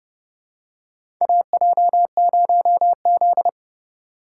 Morse code, heard as a keyed tone, about 31 wpm, 700 Hz. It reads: AJ0Z
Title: USA Call Signs - 4 Characters - 4x Character Spacing - 30wpm